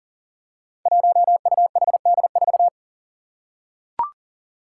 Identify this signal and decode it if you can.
Morse code, heard as a keyed tone, about 40 wpm, 700 Hz. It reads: 1UHD4
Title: Five Challenging Letter-Number Combinations 40wpm